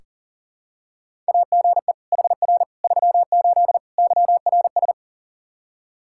Morse code, heard as a keyed tone, about 40 wpm, 700 Hz. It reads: AGE HR 38 YRS
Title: Practice QSO - Exchange 3 - More Info - 40wpm